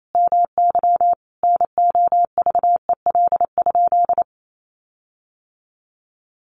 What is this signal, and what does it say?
Morse code, heard as a keyed tone, about 28 wpm, 700 Hz. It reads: MY NOVEL?
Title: Sentences from Top 1300 Words 28wpm